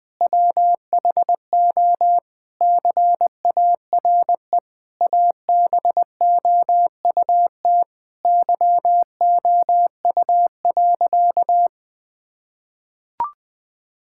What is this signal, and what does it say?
Morse code, heard as a keyed tone, about 20 wpm, 700 Hz. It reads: WHO CARE ABOUT YOU.
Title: Sentences with 7-Word Phrases - 20wpm